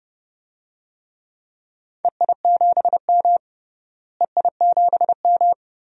Morse code, heard as a keyed tone, about 30 wpm, 700 Hz. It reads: EI7M EI7M
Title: DXCC-100 Call Signs - 4 Characters - Repeated 3x - 30wpm